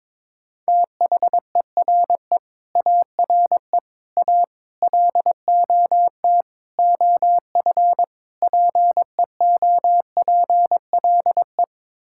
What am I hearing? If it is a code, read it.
Morse code, heard as a keyed tone, about 22 wpm, 700 Hz. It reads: THERE ARE A LOT OF PEOPLE
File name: practice / 22wpm / Sets of 6 Words - 22wpm